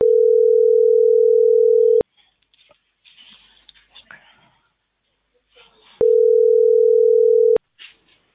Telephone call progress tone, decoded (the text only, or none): ringback tone